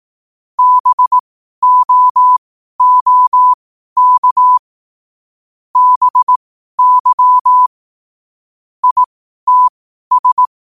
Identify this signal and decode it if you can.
Morse code, heard as a keyed tone, about 18 wpm, spaced as for 12 wpm, 1000 Hz. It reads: BOOK BY ITS
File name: qotdfiles / 12wpm